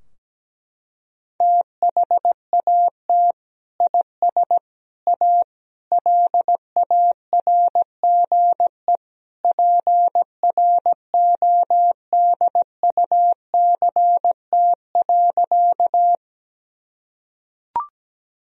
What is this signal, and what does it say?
Morse code, heard as a keyed tone, about 17 wpm, 700 Hz. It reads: THAT IS A LARGE PRODUCT.
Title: Sentences from Top 1000 Words - Review Part 2 - 17wpm